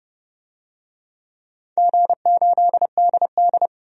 Morse code, heard as a keyed tone, about 32 wpm, 700 Hz. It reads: G8DD
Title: DXCC-100 Call Signs - 4 Characters - 30wpm